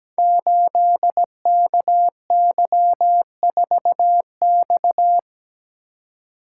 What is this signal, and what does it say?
Morse code, heard as a keyed tone, about 17 wpm, 700 Hz. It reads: 8KY4X
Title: Code Groups - 5 Letter:Numbers - 17wpm